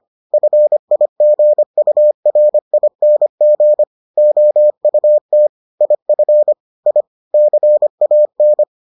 Morse code, heard as a keyed tone, about 25 wpm, 600 Hz. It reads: FIGURING OUT IF I CAN